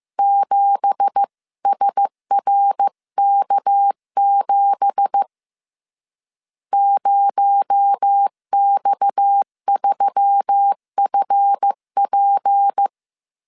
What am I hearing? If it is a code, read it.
Morse code, heard as a keyed tone, about 15 wpm, 800 Hz. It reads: 7SRK7 0X3FP
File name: CW / DET / 1 9 1500C S C 14wpm 3245